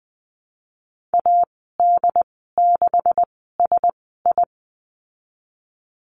Morse code, heard as a keyed tone, about 20 wpm, spaced as for 14 wpm, 700 Hz. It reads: AD6SI
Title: 147 Lesson 24g - Letter y - Call Signs 20wpm